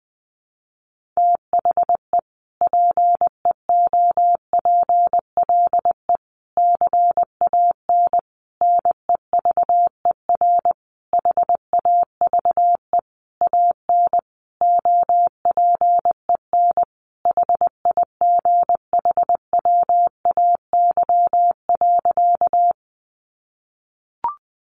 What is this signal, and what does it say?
Morse code, heard as a keyed tone, about 19 wpm, 700 Hz. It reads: THE PEOPLE CAN NEVER HAVE AN OPEN HIGHWAY.
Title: Sentences from Top 1400 Words 20wpm